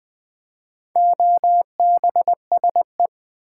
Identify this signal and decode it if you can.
Morse code, heard as a keyed tone, about 20 wpm, 700 Hz. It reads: OBSE
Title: Build A Word - 4 Letters At A Time - 11-Letter Words - 20wpm